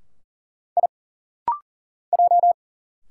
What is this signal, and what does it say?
Morse code, heard as a keyed tone, about 40 wpm, 700 Hz. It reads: I J
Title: Single Letters - Rapid-Fire 40wpm